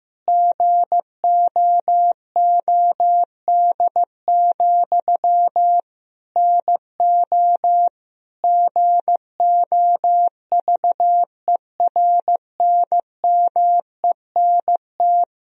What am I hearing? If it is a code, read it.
Morse code, heard as a keyed tone, about 16 wpm, 700 Hz. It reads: GOOD, NO GOVERNMENT
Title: Sentences from Top 1000 Words - Review Part 3 - 15wpm